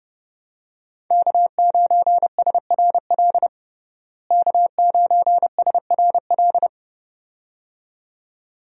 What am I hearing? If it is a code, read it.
Morse code, heard as a keyed tone, about 30 wpm, 700 Hz. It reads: K9SRL K9SRL
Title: USA Call Signs - 5 Characters - Repeated 2x - 30wpm